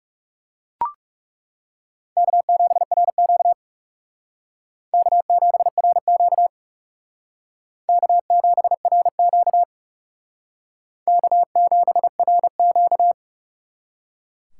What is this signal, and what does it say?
Morse code, heard as a keyed tone, about 40 wpm, 700 Hz. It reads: K7RQ K7RQ K7RQ K7RQ